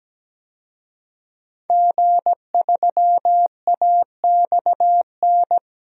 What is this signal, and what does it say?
Morse code, heard as a keyed tone, about 17 wpm, 700 Hz. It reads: G3AXN
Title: DXCC-100 Call Signs - 5 Characters - Repeated 3x - 17wpm